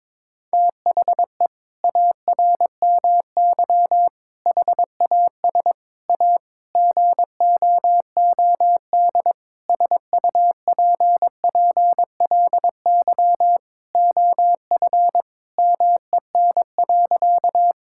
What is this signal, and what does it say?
Morse code, heard as a keyed tone, about 22 wpm, 700 Hz. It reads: THE ARMY HAS A GOOD SUPPLY OF MEN.
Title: Sentences from Top 1000 Words - Review Part 5 - 22wpm